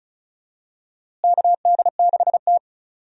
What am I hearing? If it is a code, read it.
Morse code, heard as a keyed tone, about 36 wpm, 700 Hz. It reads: KD6T
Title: USA Call Signs - 4 Characters - Repeated 2x - 35wpm